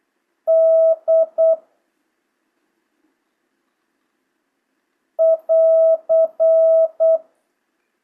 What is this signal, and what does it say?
Morse code, heard as a keyed tone, about 8 wpm, 650 Hz. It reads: D <AR>
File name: jh8cbh / 2025cw